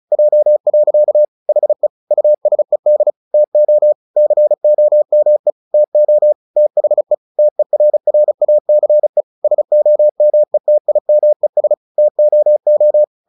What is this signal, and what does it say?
Morse code, heard as a keyed tone, about 35 wpm, 600 Hz. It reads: J. HE USED TO COME TO THE TERRACE SOMETIMES TOO
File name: the old man and the sea / the old man and the sea 1 35wpm